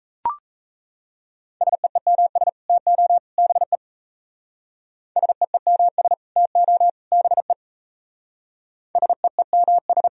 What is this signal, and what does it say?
Morse code, heard as a keyed tone, about 33 wpm, 700 Hz. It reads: SEEMS TO BE SEEMS TO BE SEEMS
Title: Speed-Racing - Sets of 3 Words 28wpm